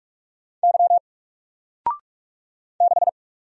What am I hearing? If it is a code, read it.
Morse code, heard as a keyed tone, about 44 wpm, 700 Hz. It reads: Y 6
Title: Single Character (Letter:Number:Punctuation:Prosign) 45wpm